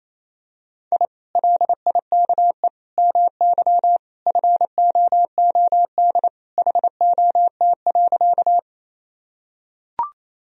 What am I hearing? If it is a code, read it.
Morse code, heard as a keyed tone, about 27 wpm, 700 Hz. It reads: I LIKE MY FOOD HOT.
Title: Sentences from Top 1000 Words - Review Part 3 - 28wpm